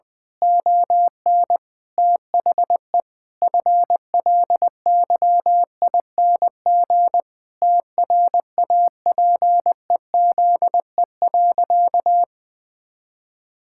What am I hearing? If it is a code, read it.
Morse code, heard as a keyed tone, about 21 wpm, 700 Hz. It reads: ON THE FLYING TRAPEZE.
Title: Pangram Challenge v2 - 20wpm